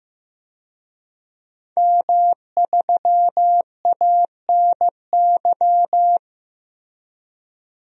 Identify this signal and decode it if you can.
Morse code, heard as a keyed tone, about 15 wpm, 700 Hz. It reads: M3ANY
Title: DXCC-100 Call Signs - 5 Characters - Repeated 3x - 15wpm